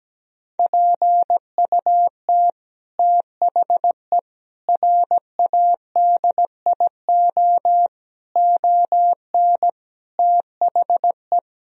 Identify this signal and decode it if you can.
Morse code, heard as a keyed tone, about 17 wpm, 700 Hz. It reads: PUT THE RADIO ON THE
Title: Sentences from Top 1000 Words - Review Part 3 - 17wpm